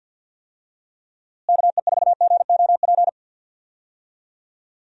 Morse code, heard as a keyed tone, about 52 wpm, 700 Hz. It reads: KE4GOP
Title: USA Call Signs - 6 Characters - Repeated 3x - 50wpm